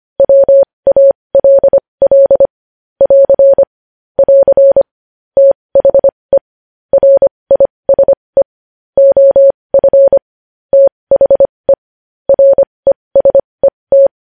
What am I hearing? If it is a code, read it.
Morse code, heard as a keyed tone, about 25 wpm, spaced as for 20 wpm, 550 Hz. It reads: WALL <AR> <AR> THE RISE OF THE RESET
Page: news headlines in morse code at 20 wpm